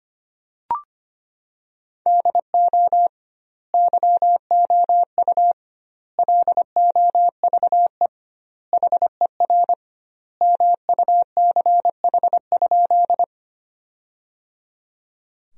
Morse code, heard as a keyed tone, about 25 wpm, 700 Hz. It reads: DO YOU LOVE HER MUCH?
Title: 2x Word-Spacing - Sentences from Top 400 Words - 25wpm